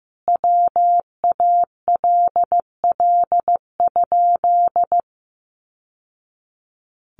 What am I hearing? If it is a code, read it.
Morse code, heard as a keyed tone, about 15 wpm, 700 Hz. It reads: WALL?
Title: Sentences from Top 1100 Words 15wpm